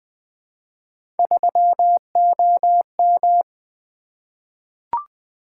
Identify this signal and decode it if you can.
Morse code, heard as a keyed tone, about 20 wpm, 700 Hz. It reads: 3OM
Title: Three Letter-Number Combinations - 20wpm